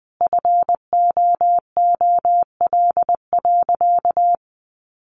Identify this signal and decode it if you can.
Morse code, heard as a keyed tone, about 20 wpm, 700 Hz. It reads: FOOL.
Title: A Sound Of Thunder 20wpm